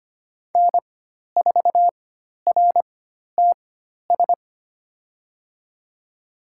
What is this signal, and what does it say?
Morse code, heard as a keyed tone, about 25 wpm, spaced as for 12 wpm, 700 Hz. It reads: N4RTS
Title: USA Call Signs - 4x Character Spacing - 25wpm